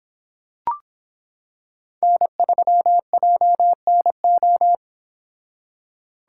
Morse code, heard as a keyed tone, about 26 wpm, 700 Hz. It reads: N3JNO